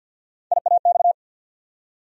Morse code, heard as a keyed tone, about 48 wpm, 700 Hz. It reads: IAX
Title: USA Call Sign Postfixes - 3 Letters 50wpm